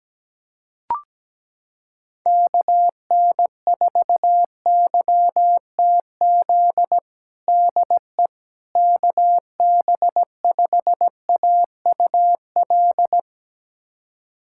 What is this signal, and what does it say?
Morse code, heard as a keyed tone, about 17 wpm, 700 Hz. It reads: KN4YTZ DE KB5AUL
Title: USA Callsign DE Callsign - 6 Characters - 17wpm